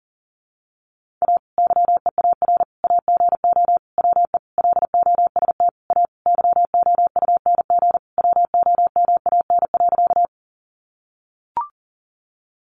Morse code, heard as a keyed tone, about 40 wpm, 700 Hz. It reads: A YEAR AGO WE LOST A YOUNG WOMAN.
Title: Sentences from Top 1400 Words 40wpm